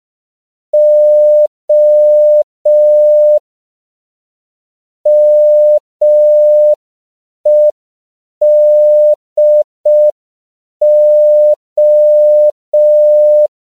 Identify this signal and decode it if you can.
Morse code, heard as a keyed tone, about 5 wpm, 600 Hz. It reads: O MEDO
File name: 5PPM